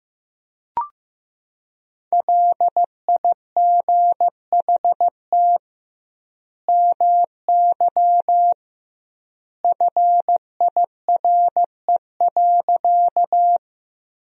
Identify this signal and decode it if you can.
Morse code, heard as a keyed tone, about 15 wpm, 700 Hz. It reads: LIGHT MY FIRE.
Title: 2x Word-Spacing - Sentences from Top 400 Words - 15wpm